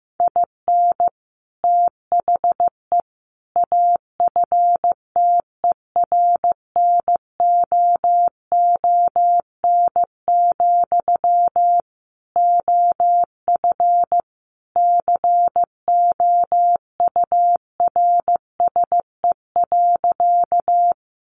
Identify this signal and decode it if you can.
Morse code, heard as a keyed tone, about 15 wpm, 700 Hz. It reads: IN THE AFTERNOON, OF COURSE.
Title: Sentences from Top 1500 Words 15wpm